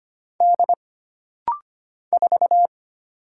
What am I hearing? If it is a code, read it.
Morse code, heard as a keyed tone, about 25 wpm, 700 Hz. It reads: D 4
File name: Single Character (Letter:Number:Punctuation:Prosign) - Rapid-Fire - 25wpm